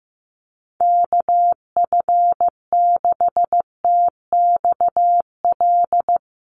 Morse code, heard as a keyed tone, about 15 wpm, 700 Hz. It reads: KF6TXL
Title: USA Call Signs Repeated 3x 15wpm